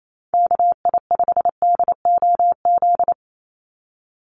Morse code, heard as a keyed tone, about 28 wpm, 700 Hz. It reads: KI5DOZ
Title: USA Call Signs 28wpm